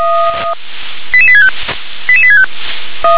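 Five-tone selective call (ZVEI), heard as quick runs of five tones, tones of about 70 ms. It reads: 88865 88865
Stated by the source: ZVEI five-tone selective call received with an SDR.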